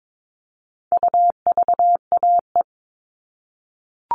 Morse code, heard as a keyed tone, about 22 wpm, 700 Hz. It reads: UVAE
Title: All Cut Numbers - 4-Digit 22wpm